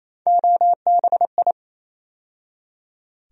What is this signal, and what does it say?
Morse code, heard as a keyed tone, about 28 wpm, 700 Hz. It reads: OBI
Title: USA Call Sign Postfixes - 3 Letters 28wpm